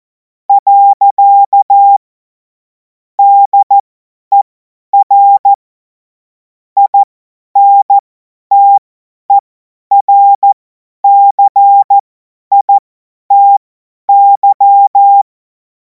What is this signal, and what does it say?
Morse code, heard as a keyed tone, about 14 wpm, spaced as for 10 wpm, 800 Hz. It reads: . DER INTERCITY
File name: Morsekurs / Witze / Witz 10. Februar 2026 50 F70